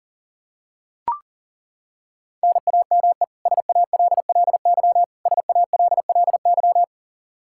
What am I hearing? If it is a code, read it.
Morse code, heard as a keyed tone, about 40 wpm, 700 Hz. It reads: NAME SALLY SALLY
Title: K1USN SST QSO Practice 40wpm